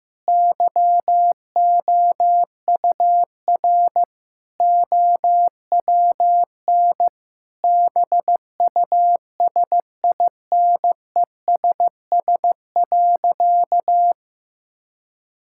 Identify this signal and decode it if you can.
Morse code, heard as a keyed tone, about 16 wpm, 700 Hz. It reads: YOUR OWN BUSINESS.
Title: Sentences from Top 1000 Words - Review Part 5 - 15wpm